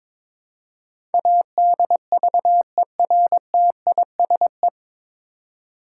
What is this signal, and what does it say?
Morse code, heard as a keyed tone, about 22 wpm, 700 Hz. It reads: ADVERTISE
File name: 9 Letter Words - First 3 Letters Given - 22wpm